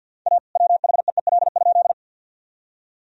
Morse code, heard as a keyed tone, about 50 wpm, 700 Hz. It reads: A WHEEL?
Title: Sentences from Top 1000 Words - Review Part 3 - 50wpm